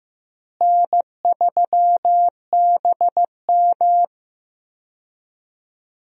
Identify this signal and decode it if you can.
Morse code, heard as a keyed tone, about 15 wpm, 700 Hz. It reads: N3BM
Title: USA Call Signs - 4 Characters - Repeated 2x - 15wpm